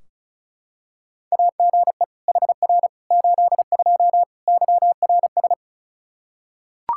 Morse code, heard as a keyed tone, about 36 wpm, 700 Hz. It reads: AGE HR 82 YRS
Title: K1USN SST QSO Practice 35wpm